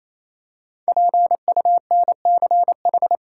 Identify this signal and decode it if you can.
Morse code, heard as a keyed tone, about 28 wpm, 700 Hz. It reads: PUNCH
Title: Five Letter Words - 28wpm